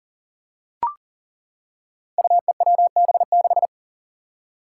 Morse code, heard as a keyed tone, about 40 wpm, 700 Hz. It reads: UEWB6
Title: Five Challenging Letter-Number Combinations 40wpm